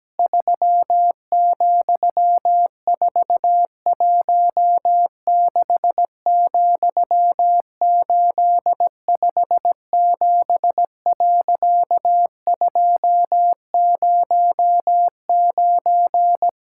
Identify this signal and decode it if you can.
Morse code, heard as a keyed tone, about 17 wpm, 700 Hz. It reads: 3,416,857.209
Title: Pangram Challenge v2 - 17wpm